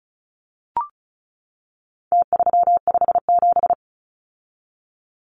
Morse code, heard as a keyed tone, about 34 wpm, 700 Hz. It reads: T357
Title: Numbers - Contest Style - 4-Digit 35wpm